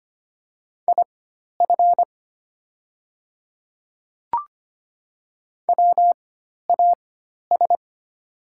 Morse code, heard as a keyed tone, about 25 wpm, spaced as for 12 wpm, 700 Hz. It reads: IF WAS
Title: 4x Character Spacing - Top 100 Words - 25wpm